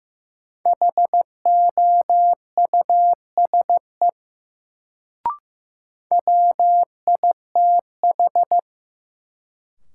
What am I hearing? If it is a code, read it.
Morse code, heard as a keyed tone, about 15 wpm, 700 Hz. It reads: HOUSE WITH
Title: Sets of 2 Words - New Word Drill - 15wpm